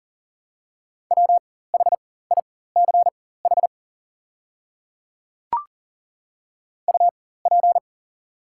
Morse code, heard as a keyed tone, about 40 wpm, spaced as for 19 wpm, 700 Hz. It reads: WHICH UP
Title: 4x Character Spacing - Top 200 Words - 40wpm